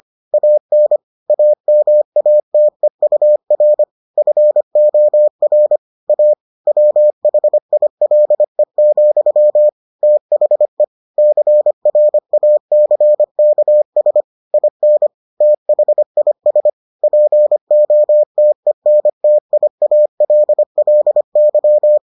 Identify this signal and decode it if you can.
Morse code, heard as a keyed tone, about 25 wpm, 600 Hz. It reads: AN AMATEUR FOR A WHILE, THE CRACKS IN THIS POTENTIALLY